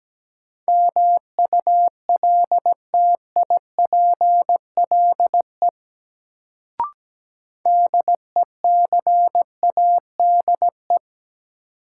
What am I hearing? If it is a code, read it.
Morse code, heard as a keyed tone, about 17 wpm, 700 Hz. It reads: MULTIPLE DECADE
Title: Sets of 5 Words - New Word Drill - 17wpm